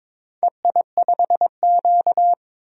Morse code, heard as a keyed tone, about 23 wpm, 700 Hz. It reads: EI5Q
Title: DXCC-100 Call Signs - 4 Characters - 22wpm